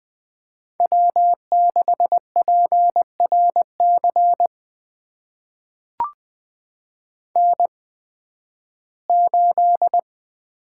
Morse code, heard as a keyed tone, about 20 wpm, spaced as for 13 wpm, 700 Hz. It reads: W6PRC N8
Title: USA Call Signs - 5 Characters - 8x Character Spacing - 20wpm